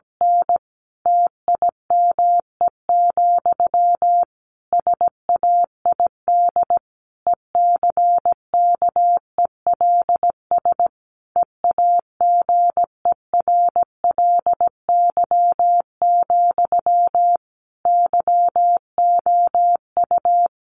Morse code, heard as a keyed tone, about 17 wpm, 700 Hz. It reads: N TIME, SAID ECKELS EAGERLY, YOU
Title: A Sound Of Thunder 17wpm